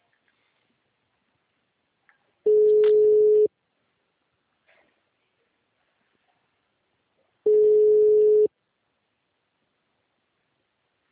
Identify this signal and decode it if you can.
Ringback tone (Europe, 425 Hz).
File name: out- -2003- - - .15748